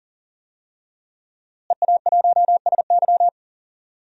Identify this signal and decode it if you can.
Morse code, heard as a keyed tone, about 40 wpm, 700 Hz. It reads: EA1SY
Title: DXCC-100 Call Signs - 5 Characters - Repeated 3x - 40wpm